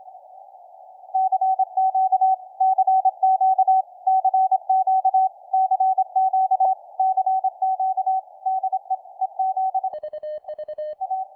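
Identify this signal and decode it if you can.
Morse code, heard as a keyed tone, about 26 wpm, 760 Hz. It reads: CQ CQ CQ CQ CQ DE EZ A
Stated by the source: Full QSK operation with a PFR-3A QRP transceiver, tested on a weak station's signal, Velko, interrupting his CQ.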